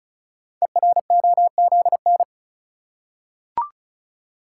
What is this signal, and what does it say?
Morse code, heard as a keyed tone, about 36 wpm, 700 Hz. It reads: EROZN
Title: Code Groups - 5 Letters - 35wpm